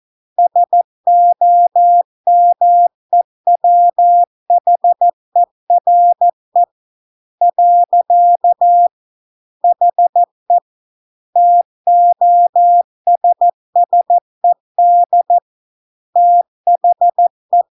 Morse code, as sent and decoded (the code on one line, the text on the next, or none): ... --- -- . .-- .... .-. / .-.-.- / .... / - --- ... ... -.. / - .... .
SOMEWHERE . HE TOSSED THE